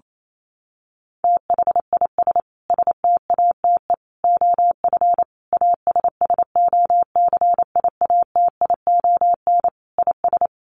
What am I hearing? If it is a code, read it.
Morse code, heard as a keyed tone, about 28 wpm, 700 Hz. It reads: THIS STATE OF ASSOCIATION IS
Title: Sentences from Top 1100 Words 28wpm